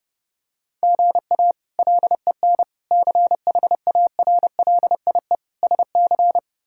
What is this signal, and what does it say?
Morse code, heard as a keyed tone, about 30 wpm, 700 Hz. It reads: GA LEN CHARLIE SC
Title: K1USN SST QSO Practice - 30wpm